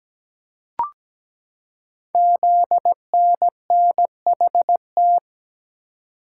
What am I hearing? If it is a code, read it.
Morse code, heard as a keyed tone, about 17 wpm, 700 Hz. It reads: ZNNHT
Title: Code Groups - 5 Letters - 17wpm